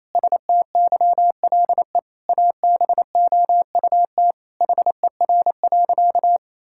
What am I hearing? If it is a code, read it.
Morse code, heard as a keyed tone, about 28 wpm, 700 Hz. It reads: STYLE ABOUT HER.
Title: Sentences from Top 1000 Words - Review Part 5 - 28wpm